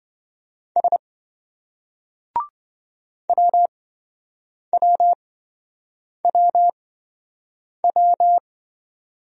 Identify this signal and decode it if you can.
Morse code, heard as a keyed tone, about 28 wpm, spaced as for 7 wpm, 700 Hz. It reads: S WWWW